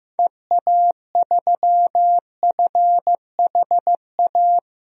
Morse code, heard as a keyed tone, about 15 wpm, 700 Hz. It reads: EA3FHA
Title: DXCC-100 Call Signs - 6 Characters - Repeated 3x - 15wpm